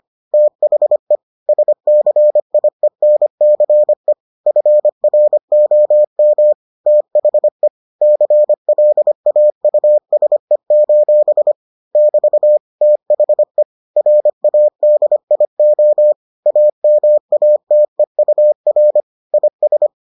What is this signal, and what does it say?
Morse code, heard as a keyed tone, about 25 wpm, 600 Hz. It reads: THE SCIENCE FROM THE CLAUSE: <BT> THE RADIO AMATEUR IS